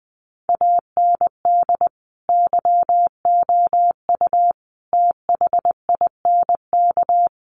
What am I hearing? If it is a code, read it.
Morse code, heard as a keyed tone, about 20 wpm, 700 Hz. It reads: AND YOU THINK
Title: Sentences from Top 1500 Words 20wpm